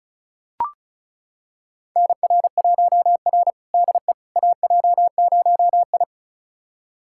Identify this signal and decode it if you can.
Morse code, heard as a keyed tone, about 36 wpm, 700 Hz. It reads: NR1R DE AJ0I
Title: USA Callsign DE Callsign - 4 Characters - 35wpm